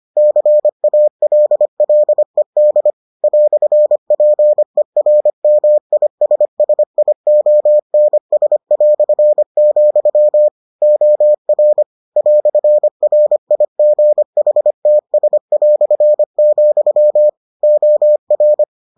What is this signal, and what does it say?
Morse code, heard as a keyed tone, about 25 wpm, 600 Hz. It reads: CALLED "PERMISSIONS", OR "RIGHTS", OR